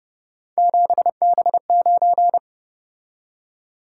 Morse code, heard as a keyed tone, about 30 wpm, 700 Hz. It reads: 7B9